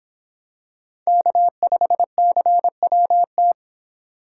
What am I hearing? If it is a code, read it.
Morse code, heard as a keyed tone, about 26 wpm, 700 Hz. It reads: K5CWT